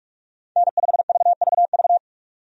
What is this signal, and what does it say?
Morse code, heard as a keyed tone, about 42 wpm, 700 Hz. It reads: N5VVV
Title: USA Call Signs - 5 Characters - 45wpm